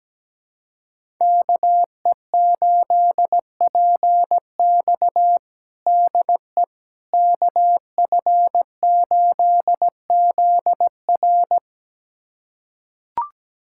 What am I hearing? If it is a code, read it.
Morse code, heard as a keyed tone, about 17 wpm, 700 Hz. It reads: KE8PX DE KF8ZR
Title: USA Callsign DE Callsign - 5 Characters - 17wpm